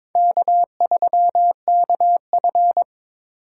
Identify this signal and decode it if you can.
Morse code, heard as a keyed tone, about 21 wpm, 700 Hz. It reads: K3KF
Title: USA Call Signs - 3x Character Spacing - 22wpm